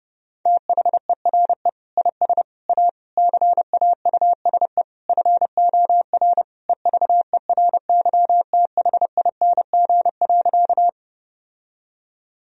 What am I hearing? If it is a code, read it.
Morse code, heard as a keyed tone, about 30 wpm, 700 Hz. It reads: THERE IS A CAUSE FOR EVERYTHING.
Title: Sentences from Top 1000 Words - Review Part 1 - 30wpm